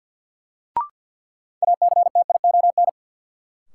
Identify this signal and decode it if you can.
Morse code, heard as a keyed tone, about 54 wpm, 700 Hz. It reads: ACTION
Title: Sentences from Top 700 Words - New Word Drill - 50wpm